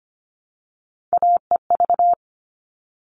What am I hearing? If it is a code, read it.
Morse code, heard as a keyed tone, about 25 wpm, 700 Hz. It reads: AEV
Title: All Cut Numbers - 3-Digit 25wpm